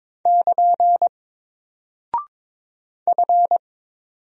Morse code, heard as a keyed tone, about 22 wpm, 700 Hz. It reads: <KN> F
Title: Single Character (Letter:Number:Punctuation:Prosign) Alt 22wpm